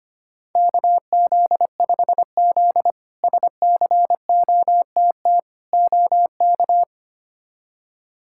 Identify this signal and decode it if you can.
Morse code, heard as a keyed tone, about 25 wpm, 700 Hz. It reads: KZ5Z SCOTT OK
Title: K1USN SST QSO Practice - 25wpm